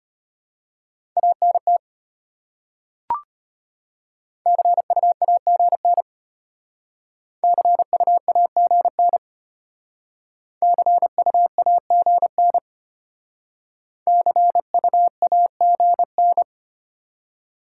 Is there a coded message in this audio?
Yes: Morse code from 1.2 to 16.4 s, heard as a keyed tone, about 33 wpm, 690 Hz. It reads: ANT CUAGN CUAGN CUAGN CUAGN